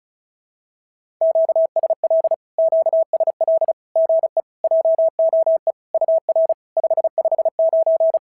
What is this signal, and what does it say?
Morse code, heard as a keyed tone, about 33 wpm, 650 Hz. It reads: QSL QSL GE JOE UR 559
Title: SOTA QSO Practice 35wpm